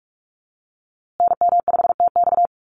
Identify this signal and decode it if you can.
Morse code, heard as a keyed tone, about 44 wpm, 700 Hz. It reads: NM5TX
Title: USA Call Signs Repeated 2x 45wpm